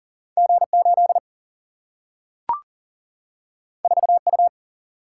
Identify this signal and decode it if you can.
Morse code, heard as a keyed tone, about 40 wpm, 700 Hz. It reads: G8 4U
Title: Single-Letter Single-Number 40wpm